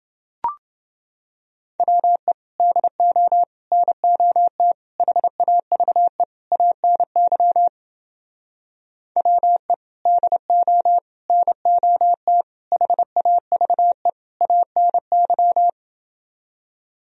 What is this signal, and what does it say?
Morse code, heard as a keyed tone, about 30 wpm, 700 Hz. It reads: WE DO NOT HAVE ANY WE DO NOT HAVE ANY